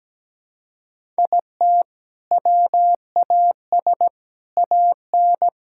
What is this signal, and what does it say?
Morse code, heard as a keyed tone, about 17 wpm, 700 Hz. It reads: IT WAS AN
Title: Sentences from Top 1000 Words - Review Part 5 - 17wpm